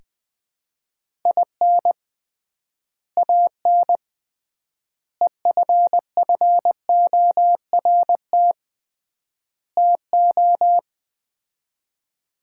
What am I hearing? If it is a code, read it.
Morse code, heard as a keyed tone, about 20 wpm, 700 Hz. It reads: IN AN EFFORT TO
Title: 3x Word-Spacing - Sets of 4 Words - 20wpm